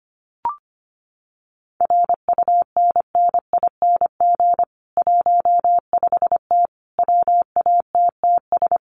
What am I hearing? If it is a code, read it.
Morse code, heard as a keyed tone, about 25 wpm, 700 Hz. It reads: RUNNING 15T WATTS
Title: Common QSO Phrases 25wpm